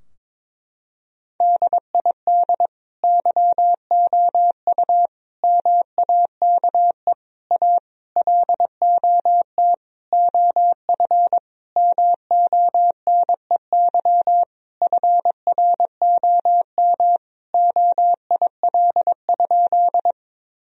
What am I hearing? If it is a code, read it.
Morse code, heard as a keyed tone, about 22 wpm, 700 Hz. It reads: DID YOU MAKE A LOT OF MONEY FROM OIL?
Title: Sentences from Top 1000 Words - Review Part 4 - 22wpm